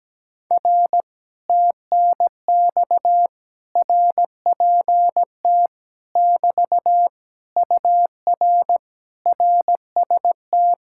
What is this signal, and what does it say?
Morse code, heard as a keyed tone, about 17 wpm, 700 Hz. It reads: R TNX RPT <BT> UR RST
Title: Speed-Racing Common QSO Phrases - 17wpm